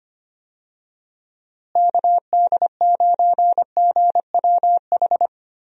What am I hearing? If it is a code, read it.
Morse code, heard as a keyed tone, about 25 wpm, 700 Hz. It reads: KD9GWH
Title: USA Call Signs - 6 Characters - Repeated 2x - 25wpm